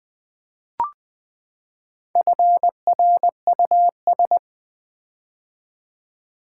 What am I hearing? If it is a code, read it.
Morse code, heard as a keyed tone, about 20 wpm, 700 Hz. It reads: FRUS
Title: Build A Word - 4 Letters At A Time - 11-Letter Words - 20wpm